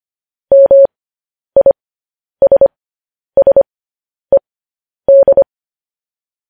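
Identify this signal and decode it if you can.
Morse code, heard as a keyed tone, about 25 wpm, spaced as for 10 wpm, 550 Hz. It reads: MISSED
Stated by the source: Morse code news headlines at 10 wpm.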